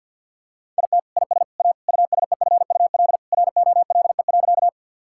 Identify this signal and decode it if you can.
Morse code, heard as a keyed tone, about 52 wpm, 700 Hz. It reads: IT IS A USEFUL ROLE.
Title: Sentences from Top 1000 Words - Review Part 6 - 50wpm